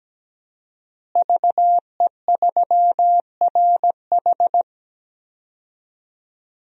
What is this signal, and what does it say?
Morse code, heard as a keyed tone, about 17 wpm, 700 Hz. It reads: VE3RH
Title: DXCC-100 Call Signs - 5 Characters - Repeated 2x - 17wpm